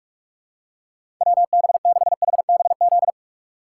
Morse code, heard as a keyed tone, about 46 wpm, 700 Hz. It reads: WB6HBZ
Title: USA Call Signs - 6 Characters - 3x Character Spacing - 45wpm